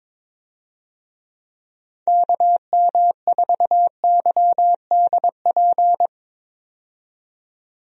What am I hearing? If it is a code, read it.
Morse code, heard as a keyed tone, about 22 wpm, 700 Hz. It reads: KM4YDP
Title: USA Call Signs - 6 Characters - 22wpm